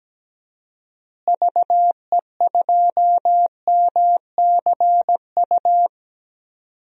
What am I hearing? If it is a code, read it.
Morse code, heard as a keyed tone, about 17 wpm, 700 Hz. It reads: VE2MCU
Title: DXCC-100 Call Signs - 6 Characters - Repeated 3x - 17wpm